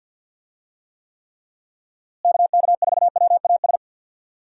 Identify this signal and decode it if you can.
Morse code, heard as a keyed tone, about 47 wpm, 700 Hz. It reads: KK4WAS
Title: USA Call Signs - 6 Characters - Repeated 3x - 50wpm